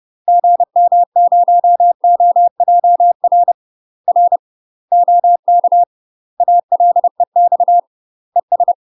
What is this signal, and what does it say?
Morse code, heard as a keyed tone, about 30 wpm, 700 Hz. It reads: GM0OJR R OK ALEX ES
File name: UK MASTER QSOs TRAINING / 30WPM / UK MASTER QSOs 0048